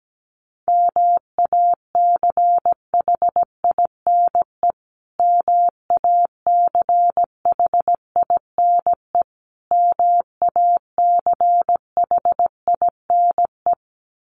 Morse code, heard as a keyed tone, about 17 wpm, 700 Hz. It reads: MACHINE MACHINE MACHINE
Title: Top 500 Words Repeated 17wpm